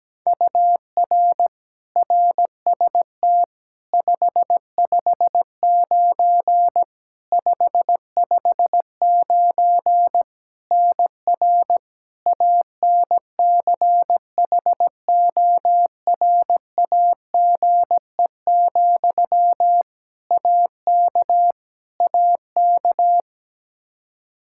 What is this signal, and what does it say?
Morse code, heard as a keyed tone, about 17 wpm, 700 Hz. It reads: UR RST 559 559 NR ANCHORAGE, AK AK
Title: Practice QSO - Exchange 1 - Beginning - 17wpm